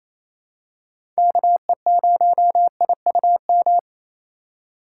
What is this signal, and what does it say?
Morse code, heard as a keyed tone, about 28 wpm, 700 Hz. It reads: KE0IUM
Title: USA Call Signs - 3x Character Spacing - 28wpm